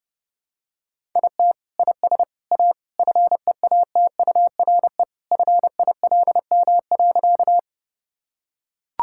Morse code, heard as a keyed tone, about 28 wpm, 700 Hz. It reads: IT IS A FEATURE FILM.
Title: Sentences from Top 1000 Words - Review Part 5 - 30wpm